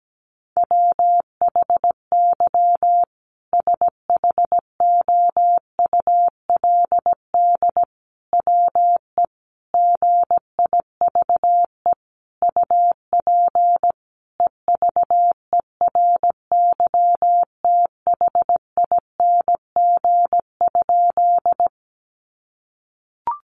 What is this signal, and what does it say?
Morse code, heard as a keyed tone, about 17 wpm, 700 Hz. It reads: WHY SHOULD WE GIVE UP EVERYTHING?
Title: Sentences from Top 700 Words 17wpm